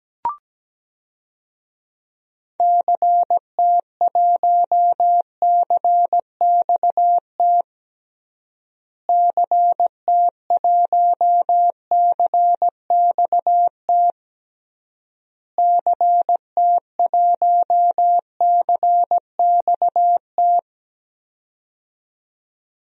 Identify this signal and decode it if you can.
Morse code, heard as a keyed tone, about 17 wpm, 700 Hz. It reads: CT1CXT CT1CXT CT1CXT
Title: DXCC-100 Call Signs - 6 Characters - Repeated 3x - 17wpm